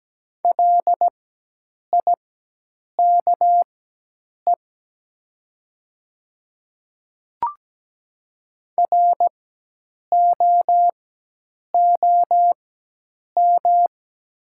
Morse code, heard as a keyed tone, about 17 wpm, spaced as for 8 wpm, 700 Hz. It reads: LIKE ROOM
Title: 4x Character Spacing - Top 500 Words - 17wpm